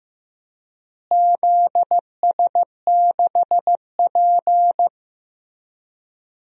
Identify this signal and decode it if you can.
Morse code, heard as a keyed tone, about 15 wpm, 700 Hz. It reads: ZS6P
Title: DXCC-100 Call Signs - 4 Characters - Repeated 3x - 15wpm